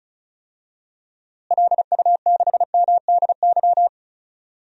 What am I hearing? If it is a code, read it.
Morse code, heard as a keyed tone, about 34 wpm, 700 Hz. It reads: LU6MDY
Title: DXCC-100 Call Signs - 6 Characters - Repeated 3x - 35wpm